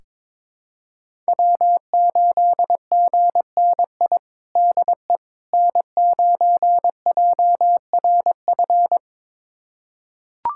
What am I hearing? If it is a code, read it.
Morse code, heard as a keyed tone, about 22 wpm, 700 Hz. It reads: W8GNI DE N9JRF
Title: USA Callsign DE Callsign - 5 Characters - 22wpm